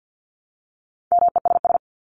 Morse code, heard as a keyed tone, about 50 wpm, 700 Hz. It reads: MESS
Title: Top 1751-2000 Words 50wpm